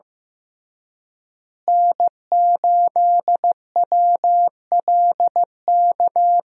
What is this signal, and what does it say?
Morse code, heard as a keyed tone, about 15 wpm, 700 Hz. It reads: N8WLK
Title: USA Call Signs - 5 Characters - Repeated 3x - 15wpm